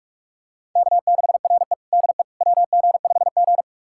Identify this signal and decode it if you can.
Morse code, heard as a keyed tone, about 46 wpm, 700 Hz. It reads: K6RE DE WG5G
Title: USA Callsign DE Callsign - 4 Characters - 45wpm